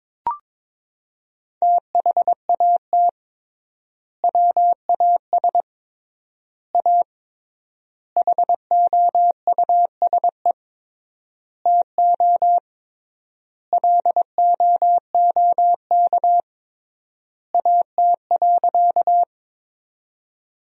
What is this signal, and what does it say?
Morse code, heard as a keyed tone, about 22 wpm, 700 Hz. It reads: THAT WAS A HOUSE TO LOOK AT.
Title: 3x Word-Spacing - Sentences from Top 200 Words - 22wpm